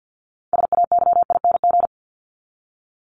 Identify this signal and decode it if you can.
Morse code, heard as a keyed tone, about 50 wpm, 700 Hz. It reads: SAYING
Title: ING Suffix - Top 100 Words with Base Words 50wpm